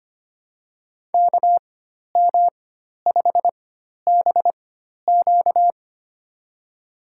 Morse code, heard as a keyed tone, about 25 wpm, spaced as for 12 wpm, 700 Hz. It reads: KM5BQ
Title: USA Call Signs - 5 Characters - 4x Character Spacing - 25wpm